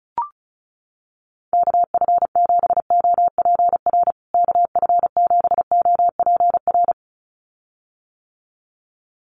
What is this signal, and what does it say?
Morse code, heard as a keyed tone, about 34 wpm, 700 Hz. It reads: KF7OPR KF7OPR
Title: USA Call Signs Repeated 2x 35wpm